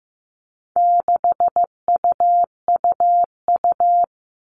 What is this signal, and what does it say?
Morse code, heard as a keyed tone, about 15 wpm, 700 Hz. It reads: 6UUU
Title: All Cut Numbers - 4-Digit 15wpm